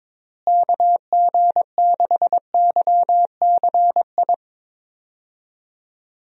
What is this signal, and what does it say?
Morse code, heard as a keyed tone, about 22 wpm, 700 Hz. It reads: KG6YCI
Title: USA Call Signs - 6 Characters - Repeated 3x - 22wpm